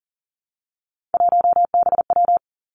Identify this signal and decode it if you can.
Morse code, heard as a keyed tone, about 38 wpm, 700 Hz. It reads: 1BW
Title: Three Challenging Letter-Number Combinations 40wpm